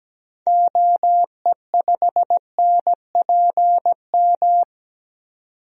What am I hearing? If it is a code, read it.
Morse code, heard as a keyed tone, about 17 wpm, 700 Hz. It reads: OE5NPM
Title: DXCC-100 Call Signs - 6 Characters - 17wpm